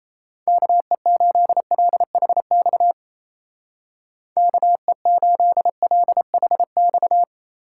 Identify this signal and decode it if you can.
Morse code, heard as a keyed tone, about 34 wpm, 700 Hz. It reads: KE8LHX KE8LHX
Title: Speed Racing - USA Call Signs 28wpm